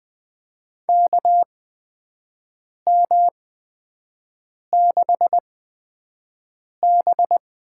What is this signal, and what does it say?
Morse code, heard as a keyed tone, about 20 wpm, spaced as for 5 wpm, 700 Hz. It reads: KM6B
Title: USA Call Signs - 8x Character Spacing - 20wpm